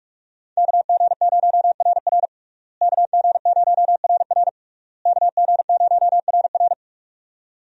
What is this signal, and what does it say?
Morse code, heard as a keyed tone, about 46 wpm, 700 Hz. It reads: KG0RR KG0RR KG0RR
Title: USA Call Signs - 5 Characters - Repeated 3x - 45wpm